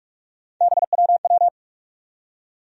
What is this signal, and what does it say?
Morse code, heard as a keyed tone, about 46 wpm, 700 Hz. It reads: BWW
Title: USA Call Sign Postfixes - 3 Letters 45wpm